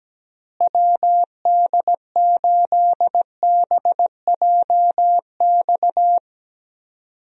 Morse code, heard as a keyed tone, about 17 wpm, 700 Hz. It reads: WD8BJX
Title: USA Call Signs - 6 Characters - Repeated 3x - 17wpm